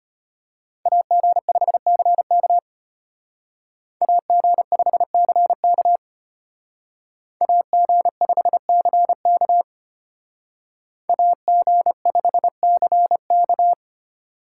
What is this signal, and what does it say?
Morse code, heard as a keyed tone, about 32 wpm, 700 Hz. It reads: AG5CK AG5CK AG5CK AG5CK